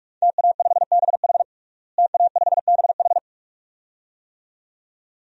Morse code, heard as a keyed tone, about 46 wpm, 700 Hz. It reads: TA5BH TA5BH
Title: DXCC-100 Call Signs - 5 Characters - Repeated 2x - 45wpm